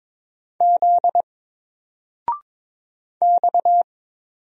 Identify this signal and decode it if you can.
Morse code, heard as a keyed tone, about 22 wpm, 700 Hz. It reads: Z X
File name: Single-Letter-Number-22wpm